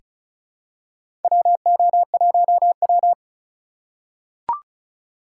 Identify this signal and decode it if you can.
Morse code, heard as a keyed tone, about 36 wpm, 700 Hz. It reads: WO1W
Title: USA Call Signs - 4 Characters - 4x Character Spacing - 35wpm